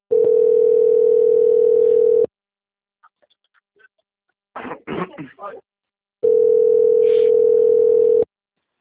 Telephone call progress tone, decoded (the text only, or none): ringback tone